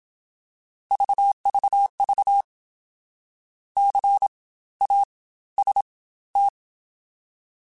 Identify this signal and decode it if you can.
Morse code, heard as a keyed tone, about 27 wpm, spaced as for 16 wpm, 800 Hz. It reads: VVV CAST